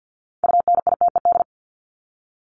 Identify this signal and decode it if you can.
Morse code, heard as a keyed tone, about 50 wpm, 700 Hz. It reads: UNITED
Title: Top 501-750 Words 50wpm